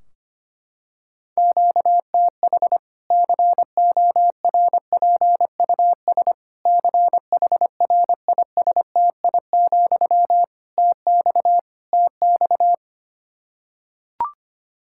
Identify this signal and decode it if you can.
Morse code, heard as a keyed tone, about 25 wpm, 700 Hz. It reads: QTH CORPUS CHRISTI, TX TX
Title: Practice QSO - Exchange 1 - Beginning - 25wpm